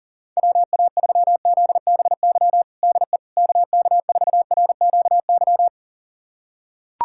Morse code, heard as a keyed tone, about 37 wpm, 700 Hz. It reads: WA3ZBY DE KK4RQY
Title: USA Callsign DE Callsign 40wpm